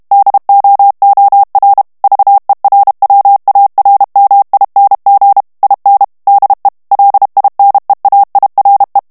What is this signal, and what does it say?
Morse code, heard as a keyed tone, about 32 wpm, 800 Hz. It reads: DOOR VERWARMING IN DE LINEAIRE
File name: HOLLANDAIS texte clair 2 vitesse 30 mots minute